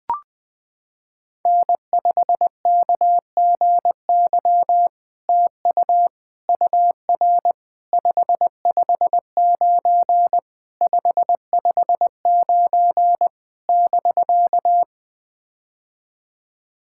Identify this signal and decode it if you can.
Morse code, heard as a keyed tone, about 20 wpm, 700 Hz. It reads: N5KGY TU UR 559 559 <BK>
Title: POTA QSO Practice - 20wpm